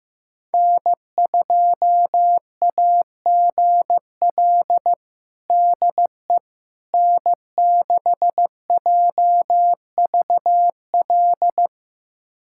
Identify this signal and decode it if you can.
Morse code, heard as a keyed tone, about 16 wpm, 700 Hz. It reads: N2AGL DE N6JVL
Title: USA Callsign DE Callsign - 5 Characters - 15wpm